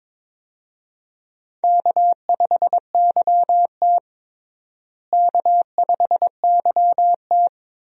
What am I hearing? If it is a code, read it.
Morse code, heard as a keyed tone, about 22 wpm, 700 Hz. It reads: K5YT K5YT
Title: USA Call Signs - 4 Characters - Repeated 2x - 22wpm